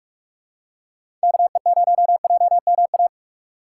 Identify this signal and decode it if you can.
Morse code, heard as a keyed tone, about 46 wpm, 690 Hz. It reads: KE0JMA
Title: USA Call Signs - 6 Characters - 3x Character Spacing - 45wpm